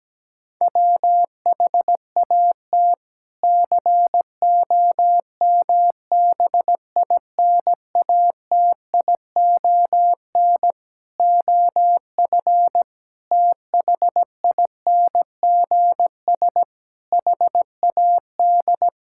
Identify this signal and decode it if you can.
Morse code, heard as a keyed tone, about 17 wpm, 700 Hz. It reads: WHAT COMBINATION OF THINGS HAD
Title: Sentences from Top 1000 Words - Review Part 6 - 17wpm